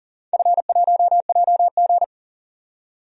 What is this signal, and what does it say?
Morse code, heard as a keyed tone, about 40 wpm, 700 Hz. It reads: F1JG
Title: DXCC-100 Call Signs - 4 Characters - 40wpm